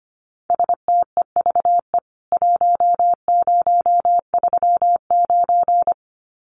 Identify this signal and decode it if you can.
Morse code, heard as a keyed tone, about 25 wpm, 700 Hz. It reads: STEVE 1039
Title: CWT Exchange 25wpm